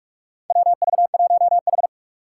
Morse code, heard as a keyed tone, about 44 wpm, 700 Hz. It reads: WV1H
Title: Four Challenging Letter-Number Combinations 45wpm